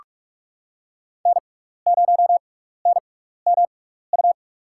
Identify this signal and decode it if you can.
Morse code, heard as a keyed tone, about 46 wpm, spaced as for 16 wpm, 700 Hz. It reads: N0NMU
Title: USA Call Signs - 6x Character Spacing - 45wpm